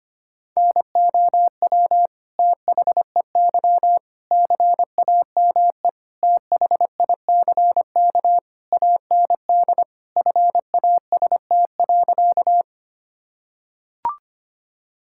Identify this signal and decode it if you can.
Morse code, heard as a keyed tone, about 25 wpm, 700 Hz. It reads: NOW THEY CAME THICK AND FAST.
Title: Sentences from Top 1600 Words 25wpm